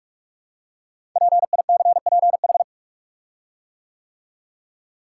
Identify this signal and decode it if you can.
Morse code, heard as a keyed tone, about 44 wpm, 700 Hz. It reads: PICPH
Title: Code Groups - 5 Letters - 45wpm